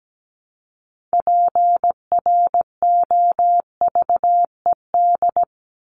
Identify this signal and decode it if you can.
Morse code, heard as a keyed tone, about 17 wpm, 700 Hz. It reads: PROVED
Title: ED Suffix - Top 100 Words with Base Words 17wpm